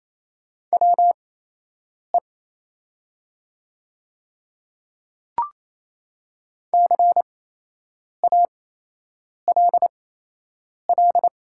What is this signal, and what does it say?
Morse code, heard as a keyed tone, about 28 wpm, spaced as for 8 wpm, 700 Hz. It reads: WE CALL
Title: 8x Character Spacing - Top 300 Words - 28wpm